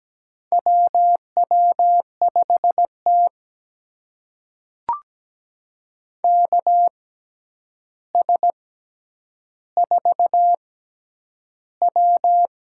Morse code, heard as a keyed tone, about 17 wpm, spaced as for 9 wpm, 700 Hz. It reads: WW5T KS4W
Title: USA Call Signs - 6x Character Spacing - 17wpm